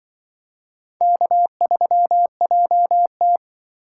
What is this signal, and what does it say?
Morse code, heard as a keyed tone, about 25 wpm, 700 Hz. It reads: K3JT
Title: Speed Racing - USA 1x2 Call Signs 20wpm